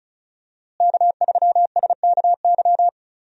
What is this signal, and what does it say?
Morse code, heard as a keyed tone, about 34 wpm, 700 Hz. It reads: K3SKY
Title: USA Call Signs - 5 Characters - Repeated 2x - 35wpm